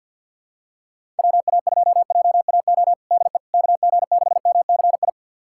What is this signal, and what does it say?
Morse code, heard as a keyed tone, about 52 wpm, 700 Hz. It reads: WA2JAO DE KG6MCI
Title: USA Callsign DE Callsign 50wpm